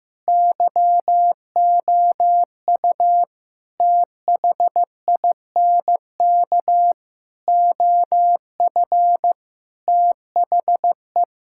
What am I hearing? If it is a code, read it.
Morse code, heard as a keyed tone, about 15 wpm, 700 Hz. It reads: YOU THINK OF THE
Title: Sentences from Top 1000 Words - Review Part 6 - 15wpm